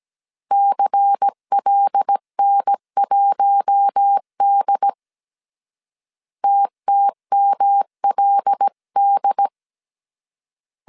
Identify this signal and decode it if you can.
Morse code, heard as a keyed tone, about 17 wpm, 800 Hz. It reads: CLN1D TTMLD
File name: CW / DET / 1 6 1500C S A 16wpm 2635